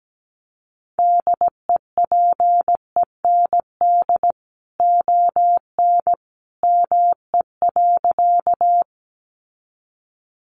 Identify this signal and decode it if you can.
Morse code, heard as a keyed tone, about 17 wpm, 700 Hz. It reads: DEPEND ON ME.
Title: Sentences from Top 1300 Words 17wpm